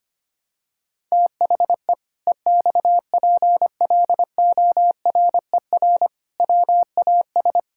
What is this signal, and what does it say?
Morse code, heard as a keyed tone, about 25 wpm, 700 Hz. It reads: THE EXPLORER WAS
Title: Pangram Challenge v2 - 25wpm